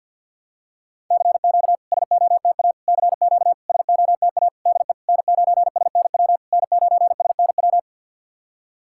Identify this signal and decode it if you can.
Morse code, heard as a keyed tone, about 49 wpm, 690 Hz. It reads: CQ SOTA CQ SOTA DE N9SNW N9SNW
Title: SOTA QSO Practice 50wpm